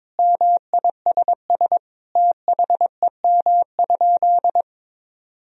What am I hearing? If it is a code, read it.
Morse code, heard as a keyed tone, about 22 wpm, 700 Hz. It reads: MISS THEM?
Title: Sentences from Top 1000 Words - Review Part 3 - 22wpm